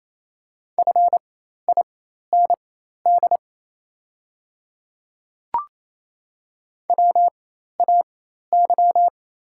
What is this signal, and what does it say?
Morse code, heard as a keyed tone, about 28 wpm, spaced as for 13 wpm, 700 Hz. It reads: FIND WAY
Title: 4x Character Spacing - Top 100 Words - 28wpm